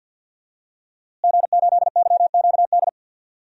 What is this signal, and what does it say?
Morse code, heard as a keyed tone, about 51 wpm, 700 Hz. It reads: G8YQD
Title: DXCC-100 Call Signs - 5 Characters - Repeated 2x - 50wpm